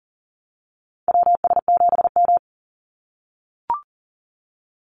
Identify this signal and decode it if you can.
Morse code, heard as a keyed tone, about 40 wpm, 700 Hz. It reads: WS7M
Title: Call Signs - 40wpm